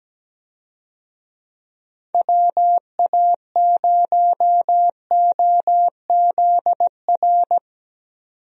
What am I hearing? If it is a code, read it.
Morse code, heard as a keyed tone, about 17 wpm, 700 Hz. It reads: WA0OZR
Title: USA Call Signs - 6 Characters - 17wpm